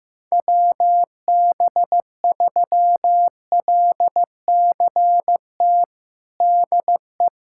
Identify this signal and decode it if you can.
Morse code, heard as a keyed tone, about 16 wpm, 700 Hz. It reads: WB3LCT DE
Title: USA Callsign DE Callsign 15wpm